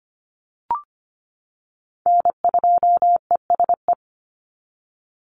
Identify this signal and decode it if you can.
Morse code, heard as a keyed tone, about 25 wpm, 700 Hz. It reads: N2ESE
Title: NA SOTA Chaser Callsigns 25wpm